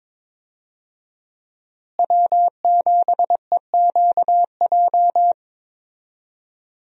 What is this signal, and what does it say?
Morse code, heard as a keyed tone, about 22 wpm, 700 Hz. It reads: W7EQJ
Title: USA Call Signs - 5 Characters - Repeated 2x - 22wpm